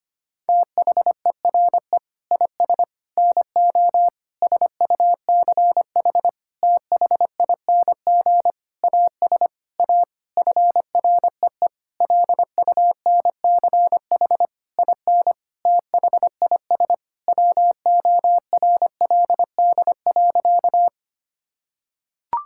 Morse code, heard as a keyed tone, about 25 wpm, 700 Hz. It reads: THERE IS NO SUCH THING AS A FREE LUNCH IN THIS WORLD.
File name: practice / 25wpm / Sentences with 9-Word Phrases - 25wpm